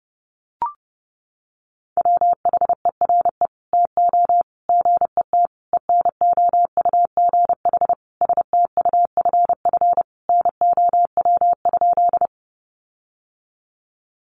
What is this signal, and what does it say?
Morse code, heard as a keyed tone, about 30 wpm, 700 Hz. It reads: WHERE TO GET ENOUGH STUFF NOW?
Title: Sentences from Top 1400 Words 30wpm